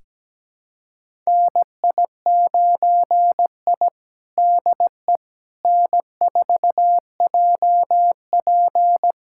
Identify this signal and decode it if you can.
Morse code, heard as a keyed tone, about 17 wpm, 700 Hz. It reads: NI9I DE N4JP
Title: USA Callsign DE Callsign - 4 Characters - 17wpm